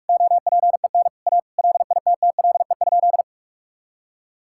Morse code, heard as a keyed tone, about 41 wpm, 700 Hz. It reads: OPEN A LITTLE?
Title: Sentences from Top 1600 Words 45wpm